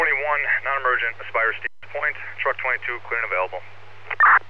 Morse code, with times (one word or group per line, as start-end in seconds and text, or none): none